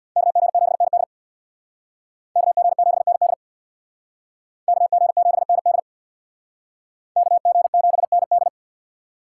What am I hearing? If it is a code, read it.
Morse code, heard as a keyed tone, about 62 wpm, 700 Hz. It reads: KG7ND KG7ND KG7ND KG7ND
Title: Speed Racing - USA Call Signs 50wpm